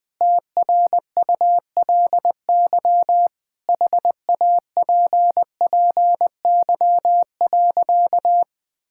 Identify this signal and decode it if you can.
Morse code, heard as a keyed tone, about 20 wpm, 700 Hz. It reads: TRULY HAPPY.
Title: Sentences with 7-Word Phrases - Spoken Beginning - 20wpm